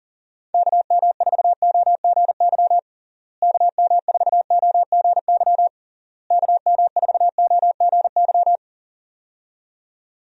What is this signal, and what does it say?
Morse code, heard as a keyed tone, about 40 wpm, 700 Hz. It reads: KM4OGY KM4OGY KM4OGY
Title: USA Call Signs - 6 Characters - Repeated 3x - 40wpm